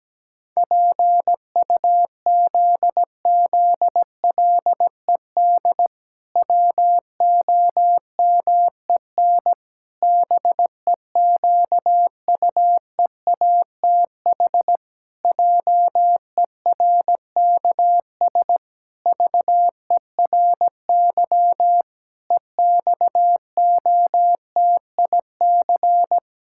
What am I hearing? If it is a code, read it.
Morse code, heard as a keyed tone, about 17 wpm, 700 Hz. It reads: PUZZLED WOMEN BEQUEATH JERKS VERY EXOTIC
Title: Pangram Challenge v2 - 17wpm